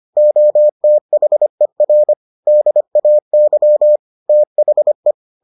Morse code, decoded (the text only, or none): OTHER DAY THE